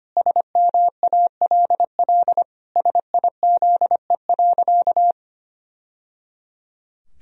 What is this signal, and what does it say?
Morse code, heard as a keyed tone, about 25 wpm, 700 Hz. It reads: SMALL SIZE.
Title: Sentences from Top 1000 Words - Review Part 2 - 25wpm